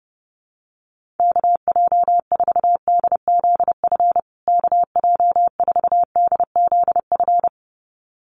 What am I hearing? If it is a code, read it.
Morse code, heard as a keyed tone, about 28 wpm, 700 Hz. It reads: KJ4DZF KJ4DZF
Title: USA Call Signs Repeated 2x 30wpm